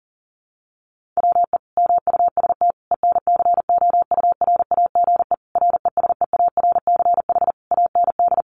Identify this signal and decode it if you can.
Morse code, heard as a keyed tone, about 40 wpm, 700 Hz. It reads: WE MUST ENCOURAGE RESEARCH AND
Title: Sentences from Top 1400 Words 40wpm